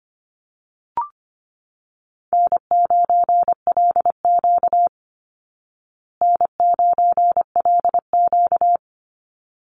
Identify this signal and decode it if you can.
Morse code, heard as a keyed tone, about 25 wpm, 700 Hz. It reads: N9LQ N9LQ
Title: Easy Call Signs Repeated 2x - 25wpm